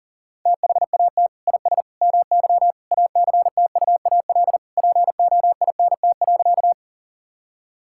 Morse code, heard as a keyed tone, about 40 wpm, 700 Hz. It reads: THAT IS MY ACTUAL POINT.
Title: Sentences from Top 1000 Words - Review Part 5 - 40wpm